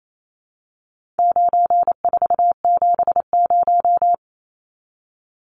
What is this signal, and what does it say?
Morse code, heard as a keyed tone, about 28 wpm, 700 Hz. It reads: 9470
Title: Numbers - 4-Digit 28wpm